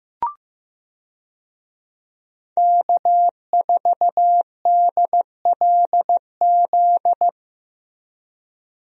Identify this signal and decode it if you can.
Morse code, heard as a keyed tone, about 15 wpm, 700 Hz. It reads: K4DLZ
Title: USA Call Signs - 5 Characters - Repeated 3x - 15wpm